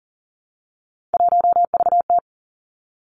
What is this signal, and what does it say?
Morse code, heard as a keyed tone, about 41 wpm, 700 Hz. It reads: 1VT
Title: Three Challenging Letter-Number Combinations 40wpm